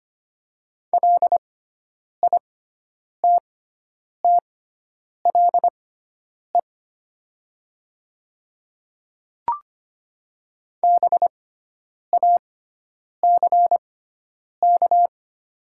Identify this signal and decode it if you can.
Morse code, heard as a keyed tone, about 25 wpm, spaced as for 9 wpm, 700 Hz. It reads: LITTLE BACK
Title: 6x Character Spacing - Top 300 Words - 25wpm